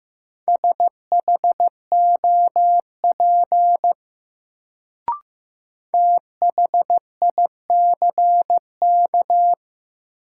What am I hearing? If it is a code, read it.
Morse code, heard as a keyed tone, about 15 wpm, 700 Hz. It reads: SHOP THICK
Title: Sentences from Top 1600 Words - New Word Drill Alt - 15wpm